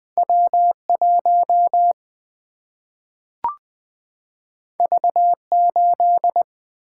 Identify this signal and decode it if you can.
Morse code, heard as a keyed tone, about 20 wpm, 700 Hz. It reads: W1 V8
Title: Single-Letter Single-Number 20wpm